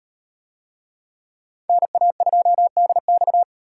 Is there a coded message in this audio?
Yes: Morse code from 1.7 to 3.4 s, heard as a keyed tone, about 38 wpm, 700 Hz. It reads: NA2DX